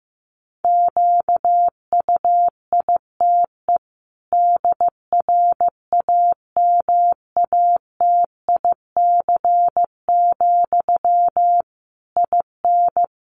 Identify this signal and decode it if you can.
Morse code, heard as a keyed tone, about 15 wpm, 700 Hz. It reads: QUITE DRAMATIC, IN
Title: Sentences from Top 1300 Words 15wpm